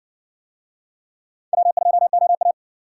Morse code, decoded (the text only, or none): W2OA